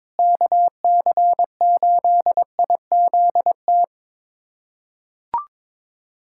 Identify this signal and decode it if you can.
Morse code, heard as a keyed tone, about 22 wpm, 700 Hz. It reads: KC8IZT
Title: USA Callsign DE Callsign 22wpm